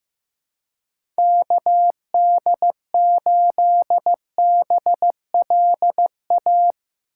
Morse code, heard as a keyed tone, about 15 wpm, 700 Hz. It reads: KD8BLA
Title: USA Call Signs - 6 Characters - 2x Character Spacing - 15wpm